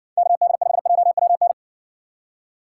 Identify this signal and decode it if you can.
Morse code, heard as a keyed tone, about 56 wpm, 700 Hz. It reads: KD5JWN